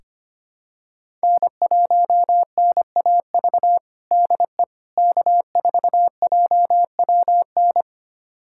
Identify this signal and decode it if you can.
Morse code, heard as a keyed tone, about 25 wpm, 700 Hz. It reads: N1NAV DE K4JWN
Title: USA Callsign DE Callsign - 5 Characters - 25wpm